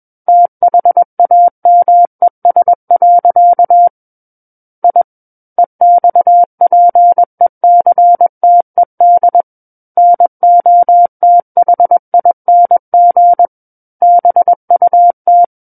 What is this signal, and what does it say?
Morse code, heard as a keyed tone, about 21 wpm, 700 Hz. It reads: THAMES. I EXPECTED NOTHING BUT
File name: Book One / 12-Chapter 12 - 21 WPM